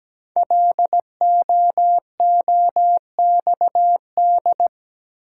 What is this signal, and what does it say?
Morse code, heard as a keyed tone, about 17 wpm, 700 Hz. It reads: LOOXD
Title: Code Groups - 5 Letter:Numbers - 17wpm